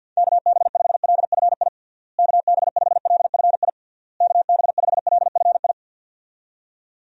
Morse code, heard as a keyed tone, about 50 wpm, 700 Hz. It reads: KB5LFI KB5LFI KB5LFI
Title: USA Call Signs - 6 Characters - Repeated 3x - 50wpm